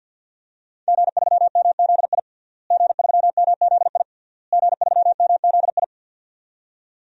Morse code, heard as a keyed tone, about 51 wpm, 700 Hz. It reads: G3MZI G3MZI G3MZI
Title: DXCC-100 Call Signs - 5 Characters - Repeated 3x - 50wpm